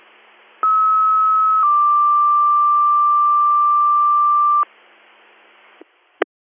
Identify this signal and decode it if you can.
Two-tone page, heard as a one-second tone, then a three-second tone, about 1290 then 1180 Hz. It is heard from 0.6 to 4.6 s.